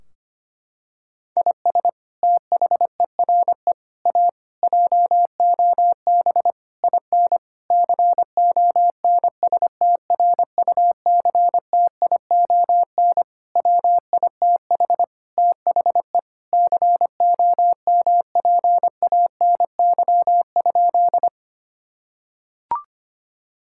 Morse code, heard as a keyed tone, about 25 wpm, 700 Hz. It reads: IS THERE A JOB IN CONSTRUCTION WITH THE COMPANY?
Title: Sentences from Top 1600 Words 25wpm